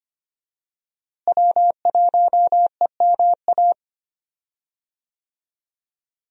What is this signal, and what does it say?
Morse code, heard as a keyed tone, about 25 wpm, 700 Hz. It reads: W1EMA
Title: USA Call Signs - 5 Characters - Repeated 3x - 25wpm